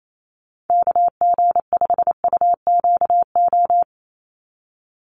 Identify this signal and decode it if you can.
Morse code, heard as a keyed tone, about 28 wpm, 700 Hz. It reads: KG5UQO
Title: USA Call Signs 28wpm